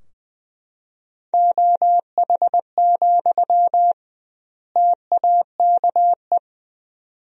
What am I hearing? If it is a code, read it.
Morse code, heard as a keyed tone, about 20 wpm, 700 Hz. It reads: OH, TAKE
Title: 2x Word-Spacing - Sentences from Top 500 Words - 20wpm